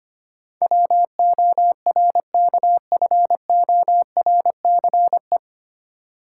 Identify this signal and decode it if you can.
Morse code, heard as a keyed tone, about 25 wpm, 700 Hz. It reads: WORKFORCE
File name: practice / 25wpm / 9 Letter Words - First 3 Letters Given - 25wpm